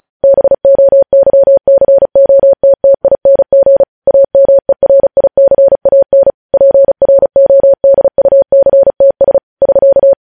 Morse code, heard as a keyed tone, about 36 wpm, 550 Hz. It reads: BOYCOTTING AMERICAN PRODUCTS <SK>
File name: podcast / News headlines-1x-35x35